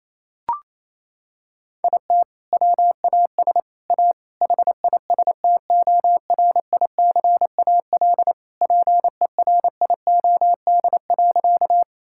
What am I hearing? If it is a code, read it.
Morse code, heard as a keyed tone, about 28 wpm, 700 Hz. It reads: IT WAS A HISTORICAL PERIOD.
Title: Sentences from Top 1000 Words - Review Part 5 - 28wpm